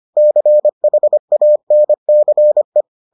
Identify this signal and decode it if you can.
Morse code, heard as a keyed tone, about 25 wpm, 600 Hz. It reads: CHANCE